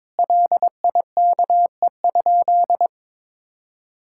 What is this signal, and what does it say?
Morse code, heard as a keyed tone, about 21 wpm, 700 Hz. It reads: LIKE?
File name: practice / 22wpm / Sentences with 3-Word Phrases - 22wpm